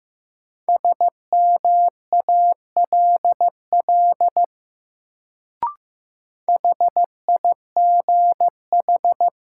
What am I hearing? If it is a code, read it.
Morse code, heard as a keyed tone, about 15 wpm, 700 Hz. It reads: SMALL HIGH
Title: Sets of 2 Words - New Word Drill - 15wpm